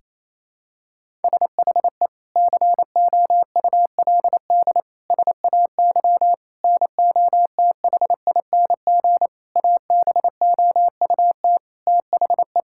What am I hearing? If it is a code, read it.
Morse code, heard as a keyed tone, about 28 wpm, 700 Hz. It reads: SHE COULD SAY NOTHING ABOUT THE
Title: Sentences from Top 1600 Words 28wpm